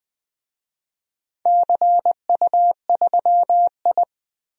Morse code, heard as a keyed tone, about 20 wpm, 700 Hz. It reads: CU3I
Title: DXCC-100 Call Signs - 4 Characters - 20wpm